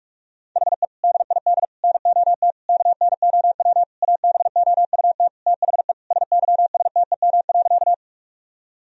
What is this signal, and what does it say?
Morse code, heard as a keyed tone, about 45 wpm, 700 Hz. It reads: HE DID NOT KNOW ABOUT THE SYSTEM.
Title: Sentences from Top 1000 Words - Review Part 3 - 45wpm